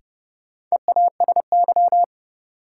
Morse code, heard as a keyed tone, about 29 wpm, 700 Hz. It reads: EASY
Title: Sentences from Top 700 Words - New Word Drill - 30wpm